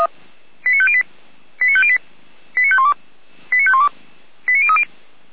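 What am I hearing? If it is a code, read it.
Five-tone selective call (ZVEI), heard as quick runs of five tones, tones of about 70 ms. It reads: 89508 89508 89512 89512 89030
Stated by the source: ZVEI five-tone selective call received via SDR.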